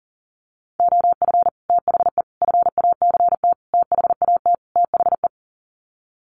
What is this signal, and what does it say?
Morse code, heard as a keyed tone, about 40 wpm, 700 Hz. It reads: OF THE FACT THAT THE
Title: Sets of 5 Words Encore2 40wpm